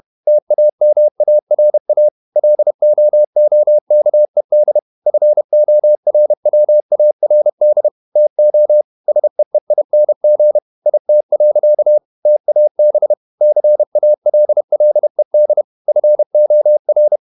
Morse code, heard as a keyed tone, about 31 wpm, 600 Hz. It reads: TAMARA LOOKED FORWARD TO SEEING IT. TAB CALLED FOR
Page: Never 31wpm | CW med Gnister